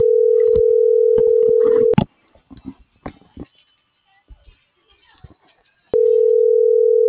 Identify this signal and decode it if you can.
Ringback tone (North America), 440 + 480 Hz.